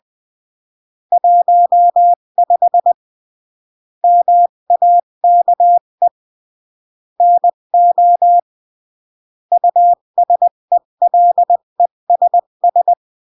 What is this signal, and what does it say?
Morse code, heard as a keyed tone, about 20 wpm, 700 Hz. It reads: 15 MAKE NO USELESS
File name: CW / jodorowsky / 82 commandments 20-15